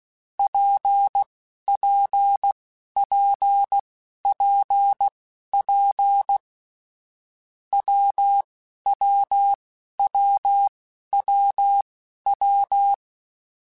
Morse code, heard as a keyed tone, about 16 wpm, spaced as for 11 wpm, 800 Hz. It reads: PPPPP WWWWW
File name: Difficult Combos / Difficult Combos-mp3s / WPJ WPJ1 @10